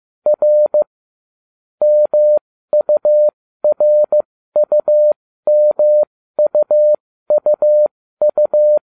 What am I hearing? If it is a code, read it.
Morse code, heard as a keyed tone, about 15 wpm, spaced as for 13 wpm, 600 Hz. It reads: R MURUMUUU